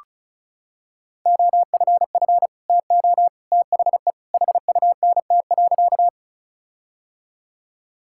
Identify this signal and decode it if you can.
Morse code, heard as a keyed tone, about 36 wpm, 700 Hz. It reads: OFF TO THE HUNT.
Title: Sentences from Top 1600 Words 35wpm